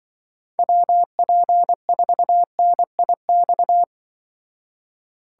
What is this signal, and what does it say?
Morse code, heard as a keyed tone, about 23 wpm, 700 Hz. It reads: WP4NIX
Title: Speed Racing - USA 2x3 Call Signs 20wpm